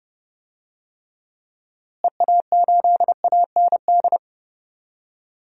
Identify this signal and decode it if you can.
Morse code, heard as a keyed tone, about 30 wpm, 700 Hz. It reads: EA8AND
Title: DXCC-100 Call Signs - 6 Characters - Repeated 2x - 30wpm